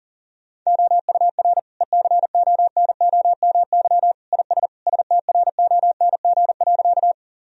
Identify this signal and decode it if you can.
Morse code, heard as a keyed tone, about 37 wpm, 700 Hz. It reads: OUR ECONOMY IS STRONG.
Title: Sentences from Top 1000 Words - Review Part 3 - 40wpm